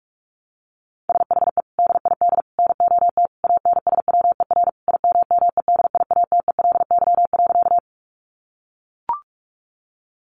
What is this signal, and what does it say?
Morse code, heard as a keyed tone, about 44 wpm, 700 Hz. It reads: SHE DID NOT ANSWER IMMEDIATELY.
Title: Sentences from Top 1100 Words 45wpm